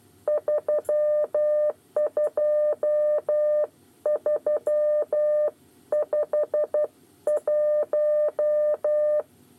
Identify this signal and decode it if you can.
Morse code, heard as a keyed tone, about 12 wpm, 600 Hz. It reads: 32351
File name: 8 bis lezione cw 2